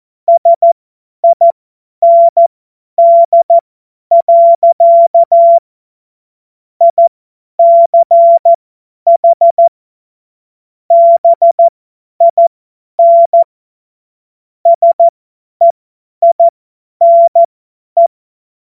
Morse code, heard as a keyed tone, about 14 wpm, spaced as for 10 wpm, 680 Hz. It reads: SIND. ICH BIN SEINE
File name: Morsekurs / Witze / Witz 07. Dezember 2021, 50 F70